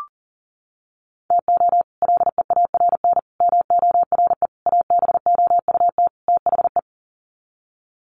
Morse code, heard as a keyed tone, about 40 wpm, 700 Hz. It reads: TO LEARN MORE ABOUT THE
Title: Sets of 5 Words Encore2 40wpm